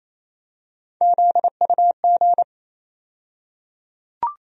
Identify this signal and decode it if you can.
Morse code, heard as a keyed tone, about 28 wpm, 700 Hz. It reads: ZUG
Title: USA Call Sign Postfixes - 3 Letters 28wpm